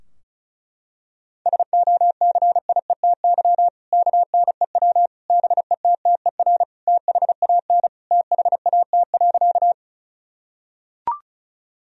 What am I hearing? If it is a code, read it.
Morse code, heard as a keyed tone, about 36 wpm, 700 Hz. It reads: SOCIETY KNEW BETTER THAN THAT.
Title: Sentences from Top 1000 Words - Review Part 4 - 35wpm